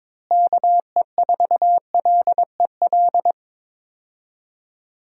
Morse code, heard as a keyed tone, about 21 wpm, 700 Hz. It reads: KE4LEL
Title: USA Call Signs - 6 Characters - Repeated 3x - 22wpm